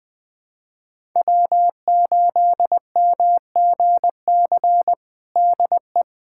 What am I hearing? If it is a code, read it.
Morse code, heard as a keyed tone, about 20 wpm, 700 Hz. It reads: W8MGC DE
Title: USA Callsign DE Callsign - 5 Characters - 20wpm